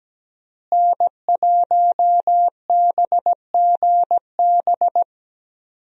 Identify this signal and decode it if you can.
Morse code, heard as a keyed tone, about 17 wpm, 700 Hz. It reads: N1BGB
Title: USA Call Signs - 5 Characters - Repeated 2x - 17wpm